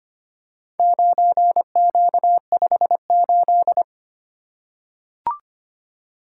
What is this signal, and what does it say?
Morse code, heard as a keyed tone, about 25 wpm, 700 Hz. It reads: 9Q58
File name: practice / 25wpm / Four Letter-Number Combinations - 25wpm